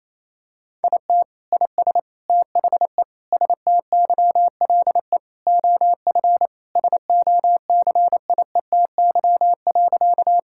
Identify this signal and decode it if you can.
Morse code, heard as a keyed tone, about 28 wpm, 700 Hz. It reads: IT IS THE STYLE OF SOCIETY.
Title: Sentences from Top 1000 Words - Review Part 5 - 28wpm